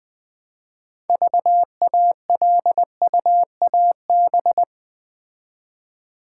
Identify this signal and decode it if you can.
Morse code, heard as a keyed tone, about 20 wpm, 700 Hz. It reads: VALUAB
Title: Build A Word - 3 Letters At A Time - 8-Letter Words - 20wpm